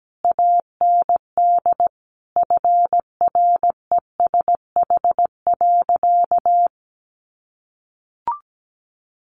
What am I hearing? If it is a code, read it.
Morse code, heard as a keyed tone, about 17 wpm, 700 Hz. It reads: AND FRESH.
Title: Sentences from Top 1200 Words 17wpm